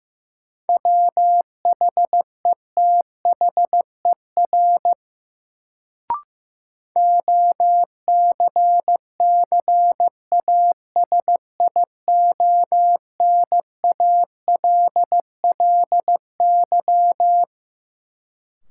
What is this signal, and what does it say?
Morse code, heard as a keyed tone, about 16 wpm, 700 Hz. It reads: WHETHER OCCASIONALLY
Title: Sentences from Top 1300 Words - New Word Drill Alt - 15wpm